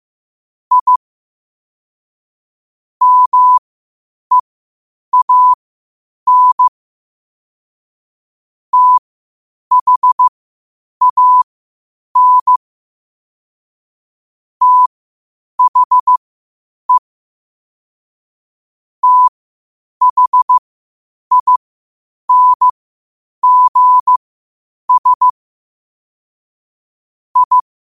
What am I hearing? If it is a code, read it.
Morse code, heard as a keyed tone, about 15 wpm, spaced as for 8 wpm, 1000 Hz. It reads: I MEAN THAN THE THINGS I
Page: New quotes every day in morse code at 8 Words per minute.